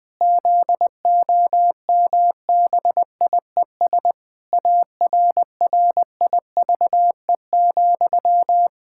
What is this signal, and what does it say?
Morse code, heard as a keyed tone, about 19 wpm, 700 Hz. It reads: ZOMBIES ARRIVE,
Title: Pangram Challenge v2 - 20wpm